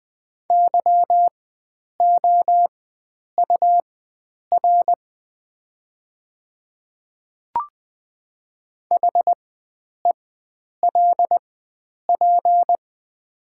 Morse code, heard as a keyed tone, about 20 wpm, spaced as for 9 wpm, 700 Hz. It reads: YOUR HELP
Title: 4x Character Spacing - Top 500 Words - 20wpm